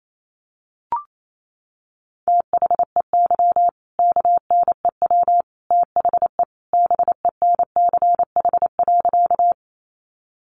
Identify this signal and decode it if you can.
Morse code, heard as a keyed tone, about 28 wpm, 700 Hz. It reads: THEY KNEW THE BENCH.
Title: Sentences from Top 1400 Words 28wpm